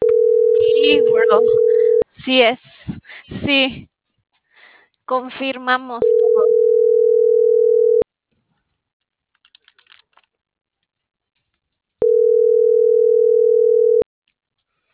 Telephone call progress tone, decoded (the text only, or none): ringback tone